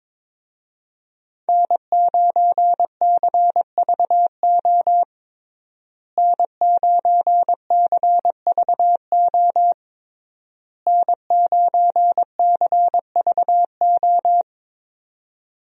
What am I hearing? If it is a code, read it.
Morse code, heard as a keyed tone, about 22 wpm, 700 Hz. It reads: N9CVO N9CVO N9CVO
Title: USA Call Signs - 5 Characters - Repeated 3x - 22wpm